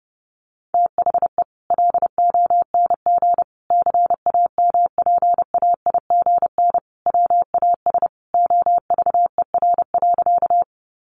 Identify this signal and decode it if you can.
Morse code, heard as a keyed tone, about 30 wpm, 700 Hz. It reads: THE LONG CAMPAIGN WAS OVER.
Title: Sentences from Top 1200 Words 30wpm